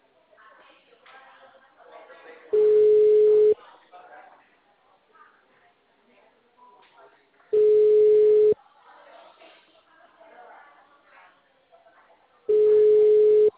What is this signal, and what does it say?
Ringback tone (Europe, 425 Hz), with speech.